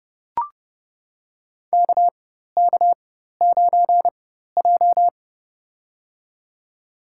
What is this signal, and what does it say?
Morse code, heard as a keyed tone, about 31 wpm, spaced as for 14 wpm, 700 Hz. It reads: KK9J
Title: USA Call Signs - 4 Characters - 4x Character Spacing - 30wpm